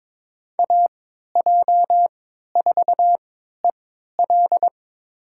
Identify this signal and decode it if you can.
Morse code, heard as a keyed tone, about 22 wpm, spaced as for 13 wpm, 700 Hz. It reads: AJ4EL
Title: USA Call Signs - 5 Characters - 3x Character Spacing - 22wpm